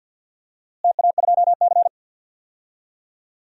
Morse code, heard as a keyed tone, about 53 wpm, 700 Hz. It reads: TA2C
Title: DXCC-100 Call Signs - 4 Characters - Repeated 3x - 50wpm